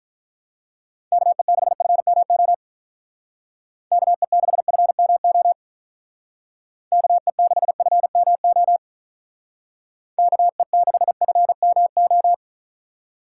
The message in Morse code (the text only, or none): KE6FMO KE6FMO KE6FMO KE6FMO